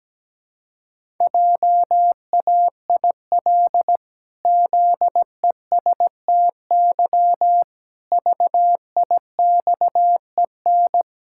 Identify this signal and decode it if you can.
Morse code, heard as a keyed tone, about 17 wpm, 700 Hz. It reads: JAIL ZESTY VIXEN
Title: Pangram Challenge v2 - 17wpm